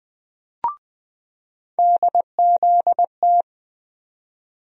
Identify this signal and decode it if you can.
Morse code, heard as a keyed tone, about 20 wpm, 700 Hz. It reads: DZT
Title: USA Call Sign Postfixes - 3 Letters 20wpm